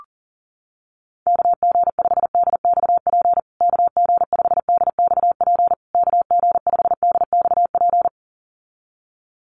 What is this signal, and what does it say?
Morse code, heard as a keyed tone, about 40 wpm, 700 Hz. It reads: KG5DXP KG5DXP KG5DXP
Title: USA Call Signs Repeated 3x 40wpm